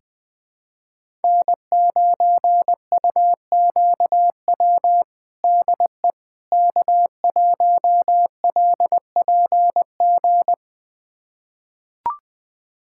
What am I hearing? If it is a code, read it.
Morse code, heard as a keyed tone, about 20 wpm, 700 Hz. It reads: N9UQW DE K1LPG
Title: USA Callsign DE Callsign - 5 Characters - 20wpm